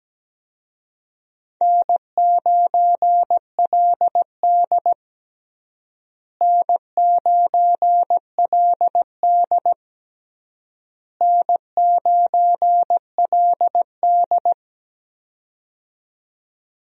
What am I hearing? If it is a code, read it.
Morse code, heard as a keyed tone, about 17 wpm, 700 Hz. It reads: N9LD N9LD N9LD
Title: USA Call Signs - 4 Characters - Repeated 3x - 17wpm